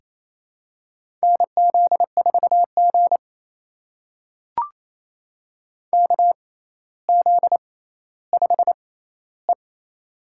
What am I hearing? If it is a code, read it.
Morse code, heard as a keyed tone, about 27 wpm, spaced as for 14 wpm, 700 Hz. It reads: NZ4G KZ5E
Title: USA Call Signs - 4 Characters - 6x Character Spacing - 28wpm